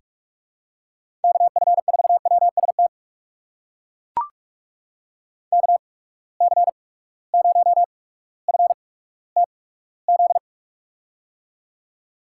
Morse code, heard as a keyed tone, about 41 wpm, spaced as for 19 wpm, 700 Hz. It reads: KF4WST KC0FTZ
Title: USA Call Signs - 6 Characters - 8x Character Spacing - 45wpm